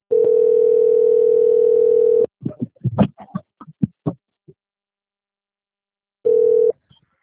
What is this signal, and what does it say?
Ringback tone (North America), 440 + 480 Hz.
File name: out- - - - - .43254